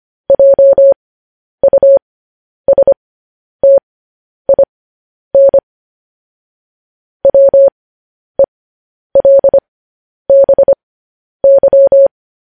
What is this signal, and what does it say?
Morse code, heard as a keyed tone, about 25 wpm, spaced as for 10 wpm, 550 Hz. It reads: JUSTIN WELBY
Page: news-headlines-in-morse-code-at-10-wpm